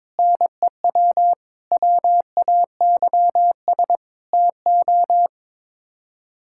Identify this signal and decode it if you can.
Morse code, heard as a keyed tone, about 22 wpm, 700 Hz. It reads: NEW WAYS TO
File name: practice / 22wpm / Sets of 6 Words - 22wpm